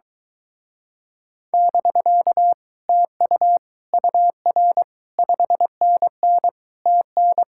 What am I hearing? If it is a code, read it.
Morse code, heard as a keyed tone, about 23 wpm, 700 Hz. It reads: <BK> TU UR 5NN TN